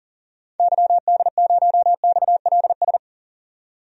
Morse code, heard as a keyed tone, about 38 wpm, 700 Hz. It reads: YD0XLS
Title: DXCC-100 Call Signs - 6 Characters - 40wpm